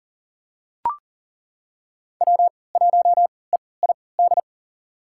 Morse code, heard as a keyed tone, about 37 wpm, spaced as for 22 wpm, 700 Hz. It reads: W1EID
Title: USA Call Signs - 5 Characters - 3x Character Spacing - 40wpm